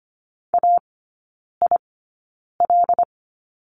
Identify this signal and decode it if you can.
Morse code, heard as a keyed tone, about 25 wpm, spaced as for 9 wpm, 700 Hz. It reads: AIL
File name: course / 25:10wpm / 093 Lesson 16e - Letter c - Three Characters 25wpm